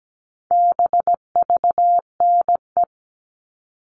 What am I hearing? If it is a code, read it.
Morse code, heard as a keyed tone, about 17 wpm, 700 Hz. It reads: BVNE
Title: All Cut Numbers - 4-Digit 17wpm